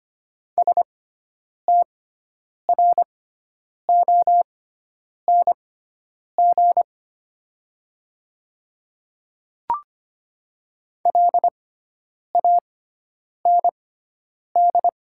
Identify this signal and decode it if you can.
Morse code, heard as a keyed tone, about 25 wpm, spaced as for 9 wpm, 700 Hz. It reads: STRONG LAND
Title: 6x Character Spacing - Top 500 Words - 25wpm